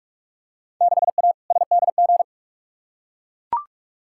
Morse code, heard as a keyed tone, about 42 wpm, 700 Hz. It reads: 6A SDG
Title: Field Day Exchanges Encore - 45wpm